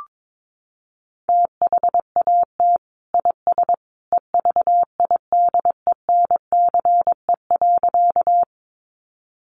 Morse code, heard as a keyed tone, about 22 wpm, 700 Hz. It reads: THAT IS EVIDENCE.
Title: Sentences from Top 1300 Words 22wpm